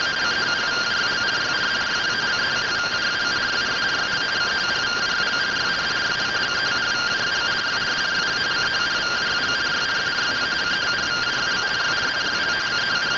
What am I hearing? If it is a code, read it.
Two-tone teleprinter signal, about 170 Hz shift.